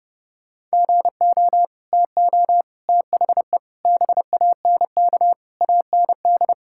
Morse code, heard as a keyed tone, about 32 wpm, 700 Hz. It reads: GO TO THE BANK AND
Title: Sentences from Top 1000 Words - Review Part 5 - 30wpm